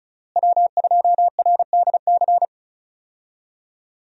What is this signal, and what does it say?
Morse code, heard as a keyed tone, about 34 wpm, 700 Hz. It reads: W2RDC
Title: USA Call Signs - 5 Characters - Repeated 3x - 35wpm